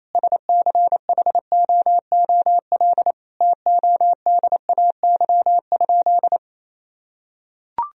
Morse code, heard as a keyed tone, about 28 wpm, 700 Hz. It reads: SCHOOL TODAY?
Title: Sentences from Top 1000 Words - Review Part 1 - 28wpm